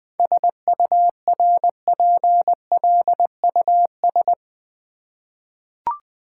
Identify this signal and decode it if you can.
Morse code, heard as a keyed tone, about 20 wpm, 700 Hz. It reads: SURPLUS
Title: 7 Letter Words - First 3 Letters Given - 20wpm